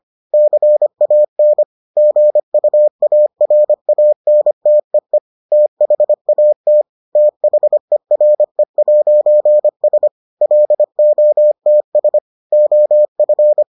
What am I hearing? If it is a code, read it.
Morse code, heard as a keyed tone, about 25 wpm, 600 Hz. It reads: CAN GUARANTEE THAT THERE'S LOTS OF